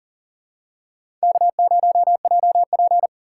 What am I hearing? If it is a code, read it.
Morse code, heard as a keyed tone, about 40 wpm, 700 Hz. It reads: K0JP
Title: USA Call Signs - 4 Characters - 6x Character Spacing - 40wpm